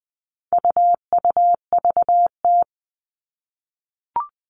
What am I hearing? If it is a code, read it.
Morse code, heard as a keyed tone, about 20 wpm, 700 Hz. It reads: UUVT
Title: All Cut Numbers - 4-Digit 20wpm